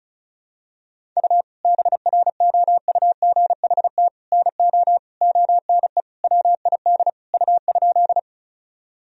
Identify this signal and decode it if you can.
Morse code, heard as a keyed tone, about 34 wpm, 700 Hz. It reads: U BROUGHT NO ONE WID U?
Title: Sentences from Top 500 Words and Abbreviations 35wpm